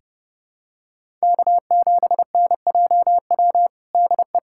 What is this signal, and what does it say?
Morse code, heard as a keyed tone, about 30 wpm, 700 Hz. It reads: K7NJW DE
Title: USA Callsign DE Callsign - 5 Characters - 30wpm